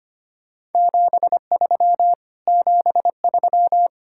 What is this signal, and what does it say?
Morse code, heard as a keyed tone, about 33 wpm, 700 Hz. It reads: 73 73